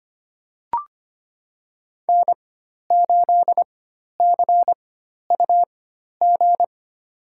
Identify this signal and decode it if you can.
Morse code, heard as a keyed tone, about 25 wpm, spaced as for 12 wpm, 700 Hz. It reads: N8CUG
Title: USA Call Signs - 4x Character Spacing - 25wpm